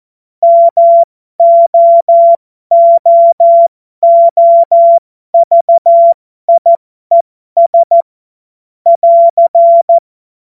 Morse code, heard as a keyed tone, about 14 wpm, spaced as for 12 wpm, 680 Hz. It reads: MOOOVIES <AR>
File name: Morsekurs / Witze